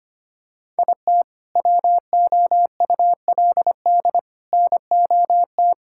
Morse code, heard as a keyed tone, about 25 wpm, 700 Hz. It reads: IT WOULD NOT
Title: Sentences from Top 1000 Words - Review Part 5 - 25wpm